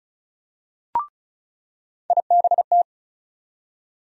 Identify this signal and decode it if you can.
Morse code, heard as a keyed tone, about 34 wpm, 700 Hz. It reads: IBT
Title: USA Call Sign Postfixes - 3 Letters 35wpm